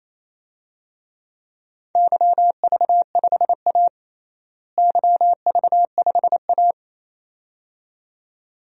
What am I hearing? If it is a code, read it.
Morse code, heard as a keyed tone, about 28 wpm, 700 Hz. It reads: YV5A YV5A
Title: DXCC-100 Call Signs - 4 Characters - Repeated 2x - 28wpm